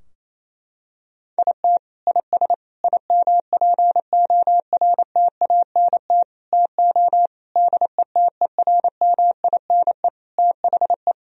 Morse code, heard as a keyed tone, about 28 wpm, 700 Hz. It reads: IT IS IMPORTANT TO DETERMINE THE
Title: Sentences from Top 1000 Words - Review Part 4 - 28wpm